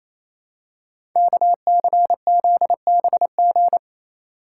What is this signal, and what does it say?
Morse code, heard as a keyed tone, about 29 wpm, 690 Hz. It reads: KCZBG
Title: Code Groups - 5 Letters - 28wpm